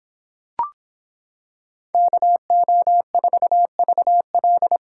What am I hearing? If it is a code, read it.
Morse code, heard as a keyed tone, about 26 wpm, 700 Hz. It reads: KO4VL